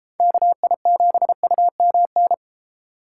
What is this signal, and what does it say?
Morse code, heard as a keyed tone, about 33 wpm, 700 Hz. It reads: KI7UMN